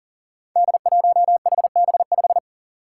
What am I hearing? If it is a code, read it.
Morse code, heard as a keyed tone, about 40 wpm, 700 Hz. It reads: D1HB5
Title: Five Challenging Letter-Number Combinations 40wpm